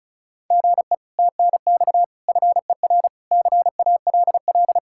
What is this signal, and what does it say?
Morse code, heard as a keyed tone, about 44 wpm, 700 Hz. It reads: GE TNX FER CALL
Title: Speed-Racing Common QSO Phrases - 35wpm